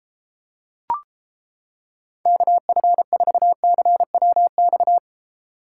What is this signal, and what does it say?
Morse code, heard as a keyed tone, about 29 wpm, 700 Hz. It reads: KF4CWX